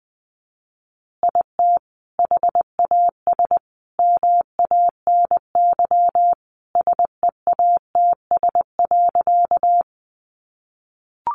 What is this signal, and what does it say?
Morse code, heard as a keyed tone, about 20 wpm, 700 Hz. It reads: IT HAS MANY SEATS.
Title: Additional Easy Phrases 20wpm